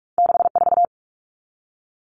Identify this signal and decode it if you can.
Morse code, heard as a keyed tone, about 46 wpm, 700 Hz. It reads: 64
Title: Common Cut Numbers - 2-Digit 45wpm